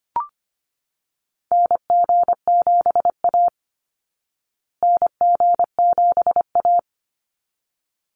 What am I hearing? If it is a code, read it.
Morse code, heard as a keyed tone, about 25 wpm, 700 Hz. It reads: NG7A NG7A
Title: Call Signs Repeated 3x - 25wpm